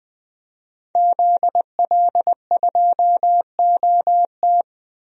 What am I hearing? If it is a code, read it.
Morse code, heard as a keyed tone, about 20 wpm, 700 Hz. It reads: ZL2OT
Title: DXCC-100 Call Signs - 5 Characters - 20wpm